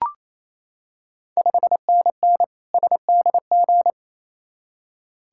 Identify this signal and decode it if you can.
Morse code, heard as a keyed tone, about 28 wpm, 700 Hz. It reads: 5NN SDG
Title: Ontario QSO Party Exchange 28wpm